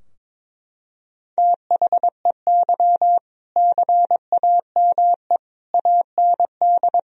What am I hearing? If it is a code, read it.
Morse code, heard as a keyed tone, about 22 wpm, 690 Hz. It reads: THEY CAME AND
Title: Sentences from Top 1000 Words - Review Part 3 - 22wpm